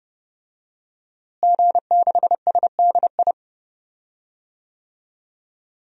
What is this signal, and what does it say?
Morse code, heard as a keyed tone, about 30 wpm, 700 Hz. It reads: G6SDI
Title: DXCC-100 Call Signs - 5 Characters - Repeated 2x - 30wpm